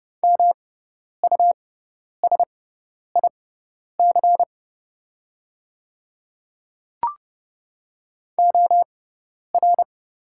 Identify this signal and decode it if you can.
Morse code, heard as a keyed tone, about 30 wpm, spaced as for 10 wpm, 700 Hz. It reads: MUSIC OR
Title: 6x Character Spacing - Top 500 Words - 30wpm